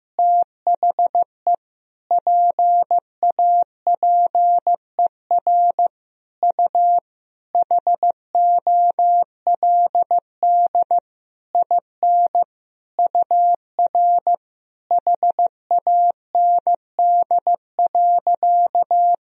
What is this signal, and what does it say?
Morse code, heard as a keyed tone, about 15 wpm, 700 Hz. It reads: THE PAPER U HOLD IN UR HAND.
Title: Sentences from Top 400 Words and Abbreviations 15wpm